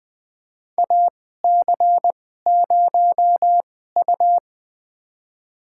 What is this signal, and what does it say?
Morse code, heard as a keyed tone, about 20 wpm, spaced as for 15 wpm, 700 Hz. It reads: AC0U
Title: USA Call Signs - 4 Characters - 2x Character Spacing - 20wpm